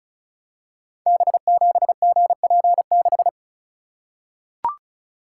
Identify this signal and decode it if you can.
Morse code, heard as a keyed tone, about 36 wpm, 700 Hz. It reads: B7GP6
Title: Code Groups - 5 Letter:Numbers - 35wpm